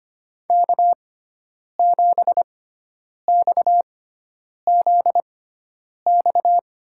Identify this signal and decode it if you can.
Morse code, heard as a keyed tone, about 25 wpm, spaced as for 9 wpm, 700 Hz. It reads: K7XZX
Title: USA Call Signs - 5 Characters - 6x Character Spacing - 25wpm